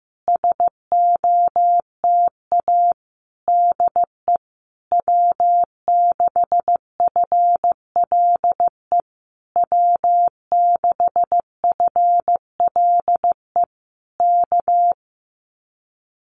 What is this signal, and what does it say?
Morse code, heard as a keyed tone, about 15 wpm, 700 Hz. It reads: SOTA DE W6FLE W6FLE K
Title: Common QSO Phrases Encore 15wpm